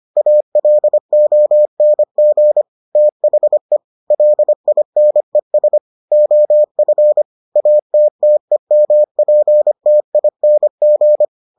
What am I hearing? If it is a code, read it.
Morse code, heard as a keyed tone, about 25 wpm, 600 Hz. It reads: ALONG THE LINES OF ATTEMPTING